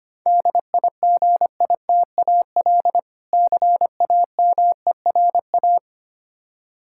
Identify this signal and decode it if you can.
Morse code, heard as a keyed tone, about 25 wpm, 700 Hz. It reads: DIGITAL CAMERA
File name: Easy Words - Speed-Racing with Hints - 25wpm